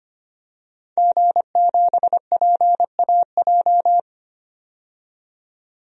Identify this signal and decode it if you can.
Morse code, heard as a keyed tone, about 25 wpm, 700 Hz. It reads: G7PAJ
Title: DXCC-100 Call Signs - 5 Characters - 25wpm